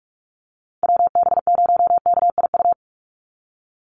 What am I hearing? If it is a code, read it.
Morse code, heard as a keyed tone, about 44 wpm, 690 Hz. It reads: WB0KIU
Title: NA SOTA Chaser Callsigns 45wpm